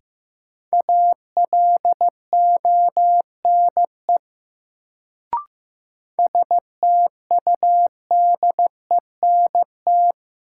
Morse code, heard as a keyed tone, about 15 wpm, 700 Hz. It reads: ALONE STUDENT
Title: Sentences from Top 1000 Words - New Word Drill - 15wpm